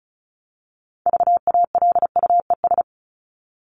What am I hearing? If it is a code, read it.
Morse code, heard as a keyed tone, about 34 wpm, 700 Hz. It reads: VALUES
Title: ES:S Suffix - Top 100 Words with Base Words 35wpm